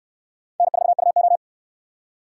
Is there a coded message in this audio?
Yes: Morse code from 0.6 to 1.4 s, heard as a keyed tone, about 68 wpm, 700 Hz. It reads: N4UO